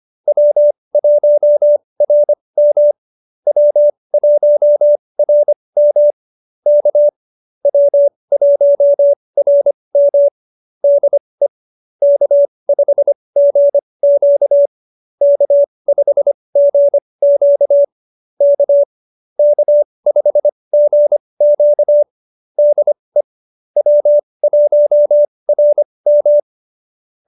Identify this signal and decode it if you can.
Morse code, heard as a keyed tone, about 25 wpm, spaced as for 20 wpm, 590 Hz. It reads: W1RM W1RM K W1RM DE K5GQ K5GQ K K5GQ DE W1RM